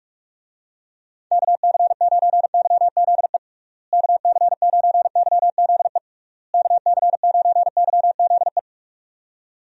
Morse code, heard as a keyed tone, about 46 wpm, 700 Hz. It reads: KC9YZE KC9YZE KC9YZE
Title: USA Call Signs - 6 Characters - Repeated 3x - 45wpm